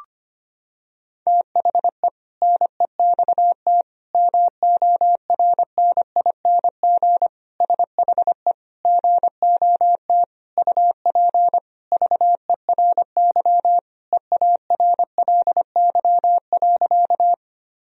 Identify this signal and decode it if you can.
Morse code, heard as a keyed tone, about 25 wpm, 700 Hz. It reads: THE NEXT MORNING SHE GOT UP VERY EARLY.
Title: Sentences from Top 1000 Words - Review Part 2 - 25wpm